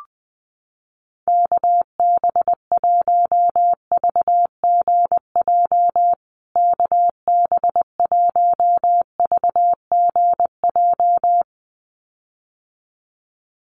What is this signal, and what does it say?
Morse code, heard as a keyed tone, about 20 wpm, 700 Hz. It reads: KB1VGJ KB1VGJ
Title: USA Call Signs Repeated 2x 20wpm